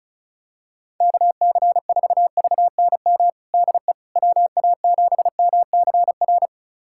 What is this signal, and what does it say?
Morse code, heard as a keyed tone, about 36 wpm, 700 Hz. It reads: KC4VNM DE WA7MCR
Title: USA Callsign DE Callsign - 6 Characters - 35wpm